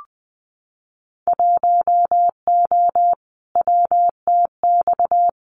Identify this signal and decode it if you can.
Morse code, heard as a keyed tone, about 20 wpm, 700 Hz. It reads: 1O WTX
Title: Winter Field Day Exchanges Encore 20wpm